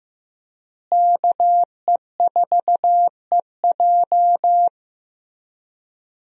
Morse code, heard as a keyed tone, about 15 wpm, 700 Hz. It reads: KE4EJ
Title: USA Call Signs - 5 Characters - Repeated 3x - 15wpm